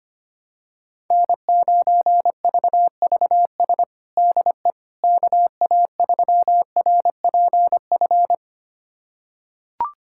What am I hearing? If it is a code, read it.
Morse code, heard as a keyed tone, about 25 wpm, 700 Hz. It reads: N9VVS DE KA3RPF
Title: USA Callsign DE Callsign 25wpm